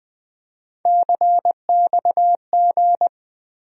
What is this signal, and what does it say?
Morse code, heard as a keyed tone, about 20 wpm, 700 Hz. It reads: CXG
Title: USA Call Sign Postfixes - 3 Letters 20wpm